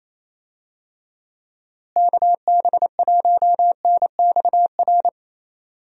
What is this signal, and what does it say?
Morse code, heard as a keyed tone, about 28 wpm, 700 Hz. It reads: KB1NXR
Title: USA Call Signs - 6 Characters - Repeated 3x - 28wpm